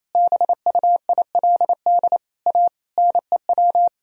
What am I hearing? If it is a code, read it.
Morse code, heard as a keyed tone, about 28 wpm, 700 Hz. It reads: BUILD A NEW
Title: Sentences from Top 1000 Words - Review Part 3 - 28wpm